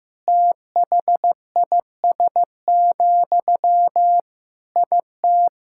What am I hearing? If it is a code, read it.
Morse code, heard as a keyed tone, about 15 wpm, 700 Hz. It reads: THIS, IT
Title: Sentences from Top 1000 Words - Review Part 6 - 15wpm